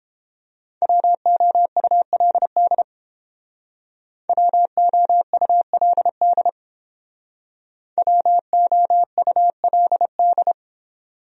Morse code, heard as a keyed tone, about 29 wpm, 700 Hz. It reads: WOULD WOULD WOULD